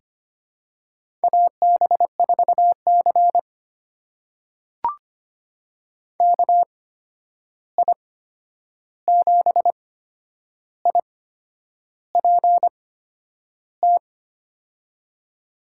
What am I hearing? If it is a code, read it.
Morse code, heard as a keyed tone, about 25 wpm, spaced as for 9 wpm, 700 Hz. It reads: AB4C KI7IPT
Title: USA Call Signs - 8x Character Spacing - 25wpm